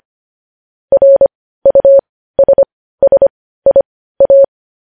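Morse code, heard as a keyed tone, about 25 wpm, spaced as for 15 wpm, 550 Hz. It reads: RUSSIA
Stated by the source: Morse practice sending news headlines at 15 wpm.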